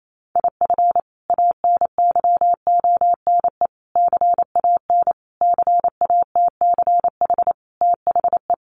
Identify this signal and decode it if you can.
Morse code, heard as a keyed tone, about 28 wpm, 700 Hz. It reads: IF ANYONE CAN CATCH THE
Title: Sentences from Top 1500 Words 28wpm